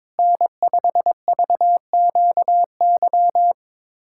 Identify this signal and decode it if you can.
Morse code, heard as a keyed tone, about 22 wpm, 700 Hz. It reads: N5VQY
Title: USA Callsign DE Callsign - 5 Characters - 22wpm